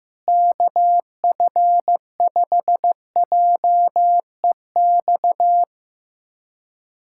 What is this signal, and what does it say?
Morse code, heard as a keyed tone, about 16 wpm, 700 Hz. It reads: KF5JEX
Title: USA Callsign DE Callsign - 6 Characters - 15wpm